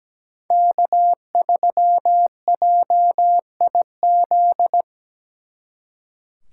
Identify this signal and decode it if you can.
Morse code, heard as a keyed tone, about 20 wpm, 700 Hz. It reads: K3JIZ
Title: Speed Racing - USA Call Signs 17wpm